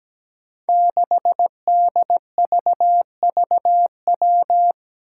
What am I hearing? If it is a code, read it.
Morse code, heard as a keyed tone, about 17 wpm, 700 Hz. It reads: 6DVVW
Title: Five Challenging Letter-Number Combinations 17wpm